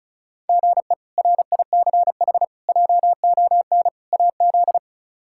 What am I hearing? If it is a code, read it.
Morse code, heard as a keyed tone, about 36 wpm, 700 Hz. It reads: GE RICH JON AZ
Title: K1USN SST QSO Practice - 35wpm